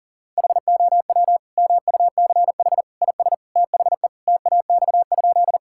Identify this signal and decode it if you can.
Morse code, heard as a keyed tone, about 40 wpm, 700 Hz. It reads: HOW MUCH IS THE TAX?
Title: Sentences from Top 1000 Words - Review Part 5 - 40wpm